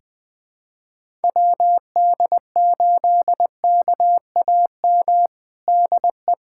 Morse code, heard as a keyed tone, about 20 wpm, 700 Hz. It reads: WD8KAM DE
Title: USA Callsign DE Callsign 20wpm